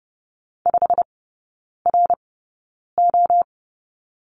Morse code, heard as a keyed tone, about 30 wpm, spaced as for 9 wpm, 700 Hz. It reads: 5RO
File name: Lesson 16e - Letter c - Three Characters 30wpm